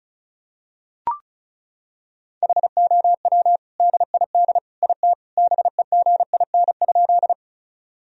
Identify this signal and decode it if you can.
Morse code, heard as a keyed tone, about 36 wpm, 700 Hz. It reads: HOW DID IT BEGIN?
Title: Sentences from Top 1000 Words - Review Part 1 - 35wpm